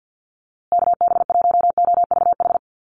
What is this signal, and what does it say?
Morse code, heard as a keyed tone, about 48 wpm, 700 Hz. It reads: KB1OVH
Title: USA Call Signs Repeated 2x 50wpm